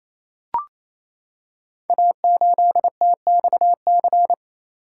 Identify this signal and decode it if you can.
Morse code, heard as a keyed tone, about 27 wpm, 700 Hz. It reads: A8TXC
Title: Code Groups - 5 Letter:Numbers - 28wpm